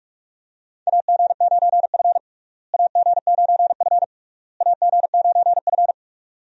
Morse code, heard as a keyed tone, about 46 wpm, 700 Hz. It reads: AG9F AG9F AG9F
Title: USA Call Signs - 4 Characters - Repeated 3x - 45wpm